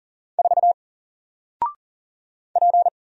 Morse code, heard as a keyed tone, about 40 wpm, 700 Hz. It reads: 4 P
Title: Single Letter-Number Alt 40wpm